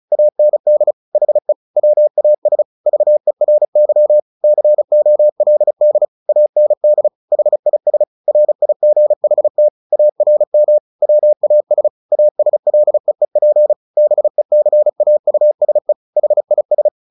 Morse code, heard as a keyed tone, about 34 wpm, 600 Hz. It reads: AND HE WAS VERY COLD AND HIS RIGHT ARM WAS ASLEEP BECAUSE HIS
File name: the old man and the sea 2 35wpm